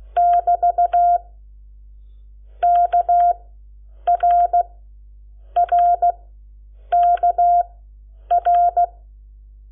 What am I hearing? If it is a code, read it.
Morse code, heard as a keyed tone, about 16 wpm, spaced as for 8 wpm, 700 Hz. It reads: <BT> KRRKR
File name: Lecciones / CLASE Nº7